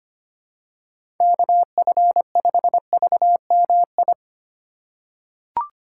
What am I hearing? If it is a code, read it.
Morse code, heard as a keyed tone, about 25 wpm, 700 Hz. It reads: KF5VMI
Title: USA Call Signs - 6 Characters - 4x Character Spacing - 25wpm